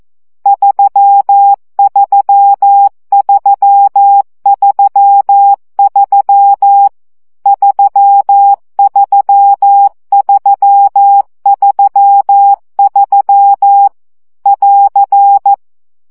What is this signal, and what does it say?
Morse code, heard as a keyed tone, about 14 wpm, 800 Hz. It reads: 33333 33333 <AR>
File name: CW gyakorlas 3 12 baud